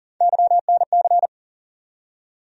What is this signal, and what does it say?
Morse code, heard as a keyed tone, about 40 wpm, 700 Hz. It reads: YNC
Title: USA Call Sign Postfixes - 3 Letters 40wpm